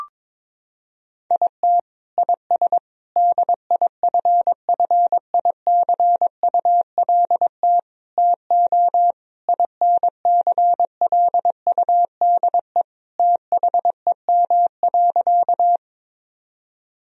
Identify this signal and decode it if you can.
Morse code, heard as a keyed tone, about 22 wpm, 700 Hz. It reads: IT IS DIFFICULT TO INCLUDE THEM.
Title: Sentences from Top 1000 Words - Review Part 3 - 22wpm